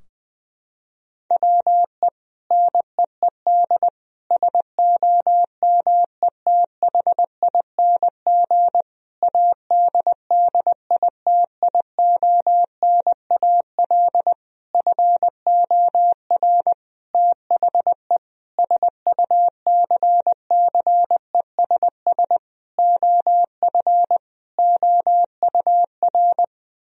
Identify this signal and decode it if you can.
Morse code, heard as a keyed tone, about 20 wpm, 700 Hz. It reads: WE NEED SOMETHING ADDITIONAL FOR THE SUCCESS OF OUR
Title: Sentences from Top 1000 Words - Review Part 4 - 20wpm